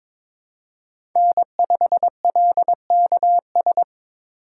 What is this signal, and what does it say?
Morse code, heard as a keyed tone, about 22 wpm, 700 Hz. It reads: N5LKS
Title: USA Call Signs - 5 Characters - 2x Character Spacing - 22wpm